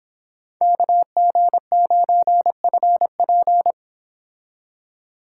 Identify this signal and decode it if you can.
Morse code, heard as a keyed tone, about 29 wpm, 700 Hz. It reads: KG9FP